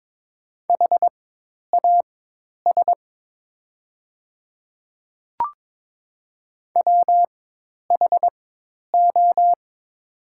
Morse code, heard as a keyed tone, about 22 wpm, spaced as for 10 wpm, 700 Hz. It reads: HAS WHO
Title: 4x Character Spacing - Top 200 Words - 22wpm